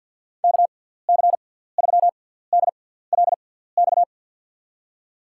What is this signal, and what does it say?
Morse code, heard as a keyed tone, about 53 wpm, spaced as for 17 wpm, 700 Hz. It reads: KC3DLX
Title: USA Call Signs - 6 Characters - 6x Character Spacing - 50wpm